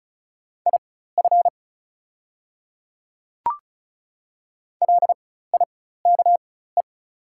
Morse code, heard as a keyed tone, about 34 wpm, spaced as for 16 wpm, 700 Hz. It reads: IF LIKE
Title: 4x Character Spacing - Top 100 Words - 35wpm